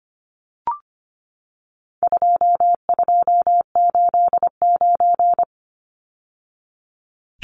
Morse code, heard as a keyed tone, about 25 wpm, 700 Hz. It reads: 2289
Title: Numbers - 4-Digit 25wpm